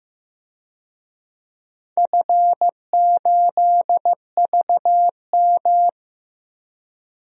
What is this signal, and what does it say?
Morse code, heard as a keyed tone, about 15 wpm, 700 Hz. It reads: F8VM
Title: DXCC-100 Call Signs - 4 Characters - Repeated 3x - 15wpm